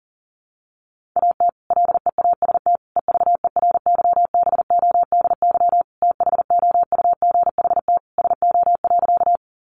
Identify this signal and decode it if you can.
Morse code, heard as a keyed tone, about 40 wpm, 700 Hz. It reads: AT LEAST EVERYBODY THOUGHT SO.
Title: Sentences from Top 1400 Words 40wpm